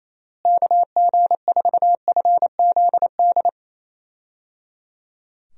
Morse code, heard as a keyed tone, about 27 wpm, 700 Hz. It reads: KG4FZD
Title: USA Call Signs - 6 Characters - Repeated 3x - 28wpm